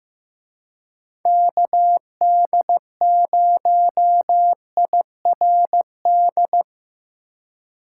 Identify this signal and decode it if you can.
Morse code, heard as a keyed tone, about 17 wpm, 700 Hz. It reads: KD0IRD
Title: Speed Racing - USA 2x3 Call Signs 15wpm